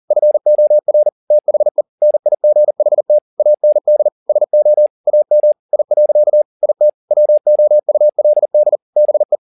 Morse code, heard as a keyed tone, about 40 wpm, 600 Hz. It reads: FOR THE NIGHT AND SO AM I. IT WOULD BE
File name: the old man and the sea 2 40wpm